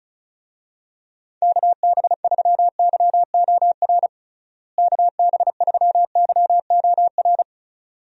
Morse code, heard as a keyed tone, about 36 wpm, 700 Hz. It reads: KB3YOR KB3YOR
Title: USA Call Signs - 6 Characters - Repeated 2x - 35wpm